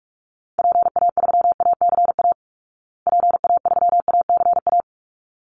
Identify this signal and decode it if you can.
Morse code, heard as a keyed tone, about 44 wpm, 700 Hz. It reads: PA3ACA PA3ACA
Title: Call Signs Repeated 3x - 45wpm